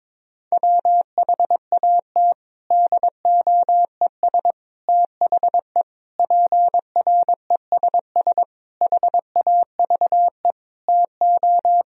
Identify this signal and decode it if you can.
Morse code, heard as a keyed tone, about 22 wpm, 700 Hz. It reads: WHAT DOES THE PRESS HAVE TO
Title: Sentences from Top 1000 Words - Review Part 3 - 22wpm